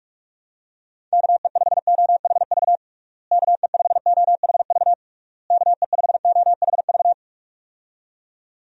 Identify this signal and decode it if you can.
Morse code, heard as a keyed tone, about 46 wpm, 700 Hz. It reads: KE5OHV KE5OHV KE5OHV
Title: USA Call Signs - 6 Characters - Repeated 3x - 45wpm